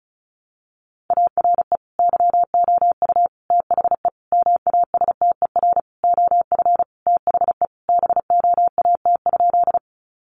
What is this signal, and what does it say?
Morse code, heard as a keyed tone, about 34 wpm, 700 Hz. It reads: ARE YOU THE MASTER OF THE BOAT?
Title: Sentences from Top 1300 Words 35wpm